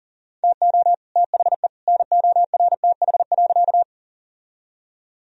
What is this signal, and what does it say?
Morse code, heard as a keyed tone, about 40 wpm, 700 Hz. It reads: TO THE NORTH.
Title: Sentences from Top 1000 Words - Review Part 1 - 40wpm